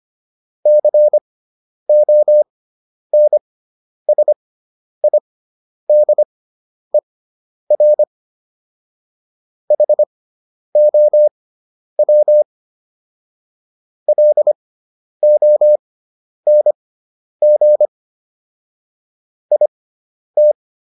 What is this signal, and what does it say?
Morse code, heard as a keyed tone, about 25 wpm, spaced as for 10 wpm, 600 Hz. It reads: CONSIDER HOW LONG IT